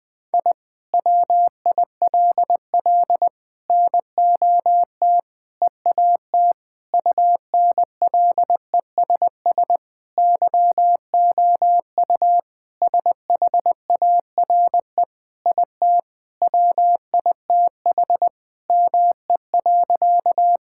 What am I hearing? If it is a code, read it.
Morse code, heard as a keyed tone, about 21 wpm, 700 Hz. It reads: I WILL NOT EAT UNLESS YOU SHARE IT WITH ME.
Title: Sentences from Top 1000 Words - Review Part 6 - 20wpm